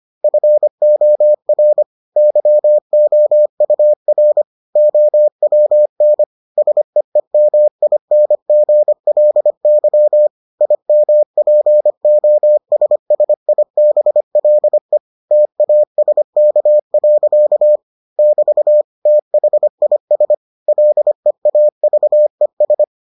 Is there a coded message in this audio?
Yes: Morse code from 0.2 to 22.8 s, heard as a keyed tone, about 25 wpm, 600 Hz. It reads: FOR YOUR OWN SEEMINGLY IMPOSSIBLE TASK. <BT> THIS LEAVES